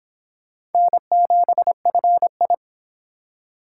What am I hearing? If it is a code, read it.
Morse code, heard as a keyed tone, about 26 wpm, 700 Hz. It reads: N7FI
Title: Speed Racing - USA Call Signs 22wpm